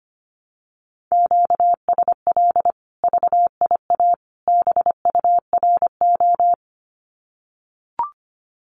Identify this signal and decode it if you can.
Morse code, heard as a keyed tone, about 25 wpm, 700 Hz. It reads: QSL VIA BURO
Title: Common QSO Phrases 25wpm